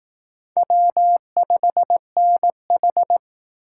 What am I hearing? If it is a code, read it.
Morse code, heard as a keyed tone, about 18 wpm, 700 Hz. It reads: W5NH
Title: Speed Racing - USA 1x2 Call Signs 15wpm